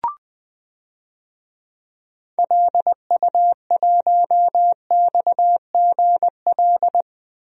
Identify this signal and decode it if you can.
Morse code, heard as a keyed tone, about 20 wpm, 700 Hz. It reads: LU1XGL
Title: DXCC-100 Call Signs - 6 Characters - Repeated 3x - 20wpm